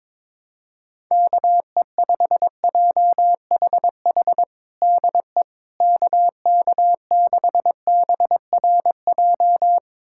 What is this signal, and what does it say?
Morse code, heard as a keyed tone, about 22 wpm, 700 Hz. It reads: KE5JHH DE KK6BRJ
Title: USA Callsign DE Callsign - 6 Characters - 22wpm